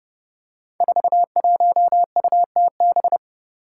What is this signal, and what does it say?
Morse code, heard as a keyed tone, about 30 wpm, 700 Hz. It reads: 41UTB
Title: Five Challenging Letter-Number Combinations 30wpm